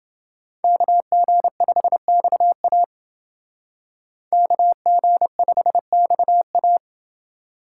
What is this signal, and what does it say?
Morse code, heard as a keyed tone, about 29 wpm, 700 Hz. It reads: KG5XA KG5XA